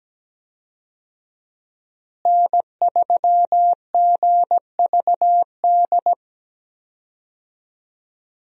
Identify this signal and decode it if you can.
Morse code, heard as a keyed tone, about 17 wpm, 700 Hz. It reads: N3GVD
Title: USA Call Signs - 5 Characters - Repeated 2x - 17wpm